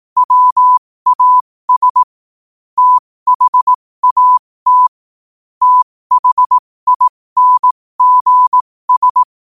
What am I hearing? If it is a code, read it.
Morse code, heard as a keyed tone, about 18 wpm, spaced as for 15 wpm, 1000 Hz. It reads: WAS THAT THINGS
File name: qotdfiles / 15wpm